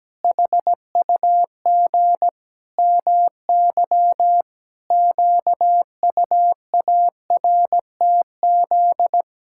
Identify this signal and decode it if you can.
Morse code, heard as a keyed tone, about 17 wpm, 700 Hz. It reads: HUG MY QUARTZ
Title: Pangram Challenge v2 - 17wpm